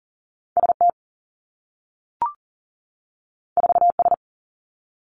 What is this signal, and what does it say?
Morse code, heard as a keyed tone, about 42 wpm, 700 Hz. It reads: S T 4 S
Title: Two Challenging Letter-Number Combinations 40wpm